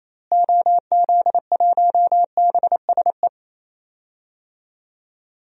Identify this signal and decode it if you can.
Morse code, heard as a keyed tone, about 28 wpm, 700 Hz. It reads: OZ1BSE
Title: DXCC-100 Call Signs - 6 Characters - Repeated 3x - 28wpm